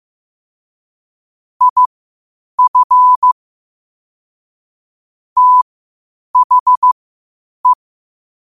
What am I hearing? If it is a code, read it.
Morse code, heard as a keyed tone, about 15 wpm, spaced as for 8 wpm, 1000 Hz. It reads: IF THE
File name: qotdfiles / 8wpm